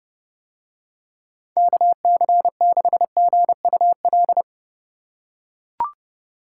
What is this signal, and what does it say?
Morse code, heard as a keyed tone, about 32 wpm, 700 Hz. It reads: KC6GUL